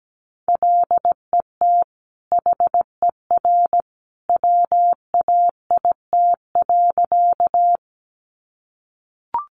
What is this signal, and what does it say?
Morse code, heard as a keyed tone, about 17 wpm, 700 Hz. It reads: LET HER WAIT.
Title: Sentences from Top 500 Words 17wpm